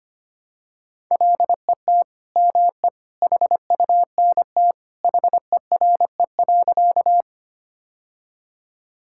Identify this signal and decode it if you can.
Morse code, heard as a keyed tone, about 25 wpm, 700 Hz. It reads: LET ME HUNT HERE.
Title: Sentences from Top 1600 Words 25wpm